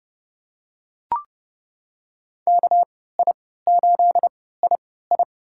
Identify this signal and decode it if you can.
Morse code, heard as a keyed tone, about 32 wpm, spaced as for 17 wpm, 700 Hz. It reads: KI8II
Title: USA Call Signs - 5 Characters - 3x Character Spacing - 30wpm